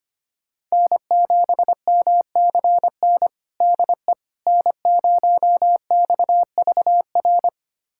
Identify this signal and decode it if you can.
Morse code, heard as a keyed tone, about 25 wpm, 700 Hz. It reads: N7MCN DE N0XVR
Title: USA Callsign DE Callsign - 5 Characters - 25wpm